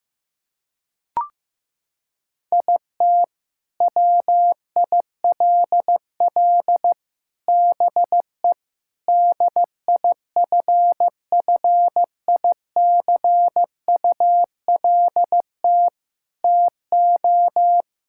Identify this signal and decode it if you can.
Morse code, heard as a keyed tone, about 15 wpm, 700 Hz. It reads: IT WILL BE DIFFICULT TO
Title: Sentences from Top 1000 Words - Review Part 6 - 15wpm